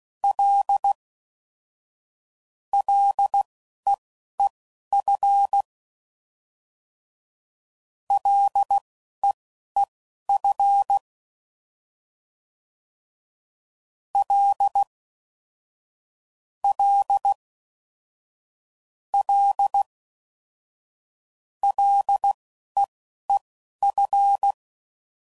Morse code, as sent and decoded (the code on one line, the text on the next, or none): .-.. / .-.. . ..-. / .-.. . ..-. / .-.. / .-.. / .-.. / .-.. . ..-.
L LEEF LEEF L L L LEEF